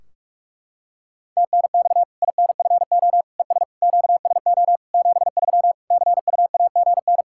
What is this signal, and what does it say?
Morse code, heard as a keyed tone, about 43 wpm, 700 Hz. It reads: TNX INFO ES QSO 73 CUAGN
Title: Practice QSO - Exchange 4 - Ending - 45wpm